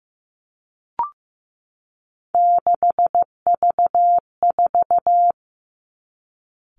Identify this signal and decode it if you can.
Morse code, heard as a keyed tone, about 15 wpm, 700 Hz. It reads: 6V4
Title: Three Challenging Letter-Number Combinations 15wpm